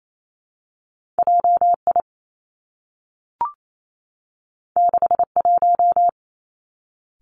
Morse code, heard as a keyed tone, about 28 wpm, 700 Hz. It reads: J I 6 1
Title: Two Challenging Letter-Number Combinations 28wpm